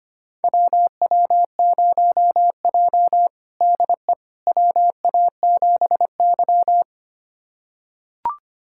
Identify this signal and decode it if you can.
Morse code, heard as a keyed tone, about 25 wpm, 700 Hz. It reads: WW0J DE WA7Y
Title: USA Callsign DE Callsign - 4 Characters - 25wpm